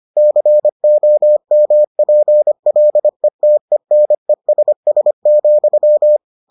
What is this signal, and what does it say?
Morse code, heard as a keyed tone, about 25 wpm, 600 Hz. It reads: COMPLETENESS,